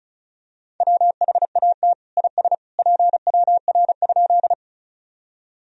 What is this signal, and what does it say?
Morse code, heard as a keyed tone, about 36 wpm, 700 Hz. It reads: WHAT IS PWR?
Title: Sentences from Top 500 Words and Abbreviations 35wpm